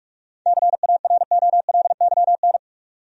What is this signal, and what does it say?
Morse code, heard as a keyed tone, about 46 wpm, 700 Hz. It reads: CAROLYN
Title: USA Common Names - 45wpm